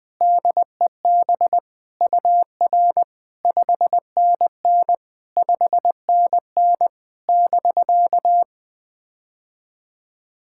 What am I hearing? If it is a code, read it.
Morse code, heard as a keyed tone, about 20 wpm, 700 Hz. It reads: DEB UR 5NN 5NN <BK>
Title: POTA QSO Practice - 20wpm